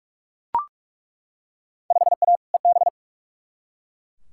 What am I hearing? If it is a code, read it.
Morse code, heard as a keyed tone, about 46 wpm, 700 Hz. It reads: 5A EB
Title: Field Day Exchanges Encore - 45wpm